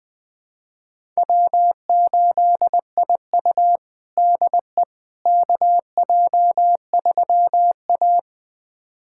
Morse code, heard as a keyed tone, about 21 wpm, 700 Hz. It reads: W8IU DE KJ3A
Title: USA Callsign DE Callsign - 4 Characters - 20wpm